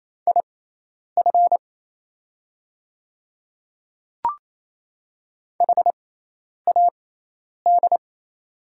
Morse code, heard as a keyed tone, about 28 wpm, spaced as for 10 wpm, 700 Hz. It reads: IF HAD
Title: 6x Character Spacing - Top 100 Words - 28wpm